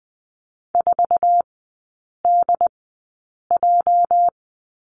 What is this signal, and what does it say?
Morse code, heard as a keyed tone, about 20 wpm, spaced as for 8 wpm, 690 Hz. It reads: 4DJ
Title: 212 Lesson 34e - Number 8 - Three Characters 20wpm